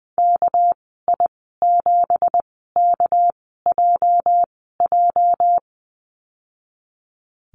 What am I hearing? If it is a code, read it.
Morse code, heard as a keyed tone, about 20 wpm, spaced as for 15 wpm, 700 Hz. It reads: KI7KJJ
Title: 213 Lesson 34g - Number 8 - Call Signs 20wpm